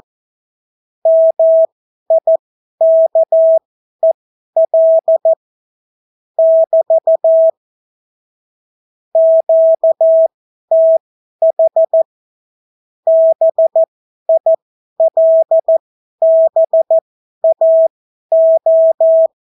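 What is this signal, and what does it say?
Morse code, heard as a keyed tone, about 14 wpm, spaced as for 11 wpm, 650 Hz. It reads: MIKEL <BT> QTH BILBAO
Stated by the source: Example Spanish CW QSO from a Morse learning book, 14 wpm characters with 8 wpm spacing.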